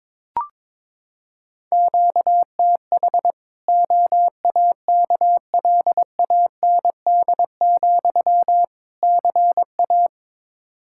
Morse code, heard as a keyed tone, about 22 wpm, 700 Hz. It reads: QTH OAKLAND, CA
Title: K1USN SST QSO Practice 22wpm